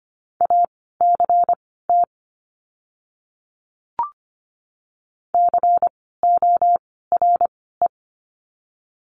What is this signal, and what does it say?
Morse code, heard as a keyed tone, about 25 wpm, spaced as for 16 wpm, 700 Hz. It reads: ACT CORE
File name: course / 25:10wpm / 094 Lesson 16f - Letter c - Words 25wpm